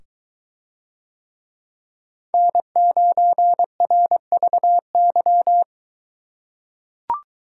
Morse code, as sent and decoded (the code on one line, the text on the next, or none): -. ----. .-. ...- -.--
N9RVY